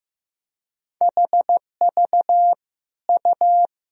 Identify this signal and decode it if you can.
Morse code, heard as a keyed tone, about 15 wpm, 700 Hz. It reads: HV U
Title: Sentences from Top 400 Words and Abbreviations 15wpm